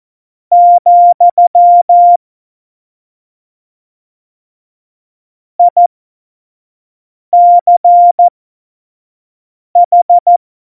Morse code, heard as a keyed tone, about 14 wpm, spaced as for 5 wpm, 700 Hz. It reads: , ICH
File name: Morsekurs / Witze / Witz 05. Dezember 2017, 25 F70